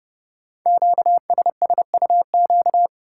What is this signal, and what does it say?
Morse code, heard as a keyed tone, about 30 wpm, 690 Hz. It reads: QSSUQ
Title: Code Groups - 5 Letters - 30wpm